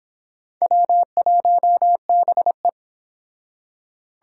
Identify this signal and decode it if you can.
Morse code, heard as a keyed tone, about 26 wpm, 700 Hz. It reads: W1BE
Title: Speed Racing - USA 1x2 Call Signs 22wpm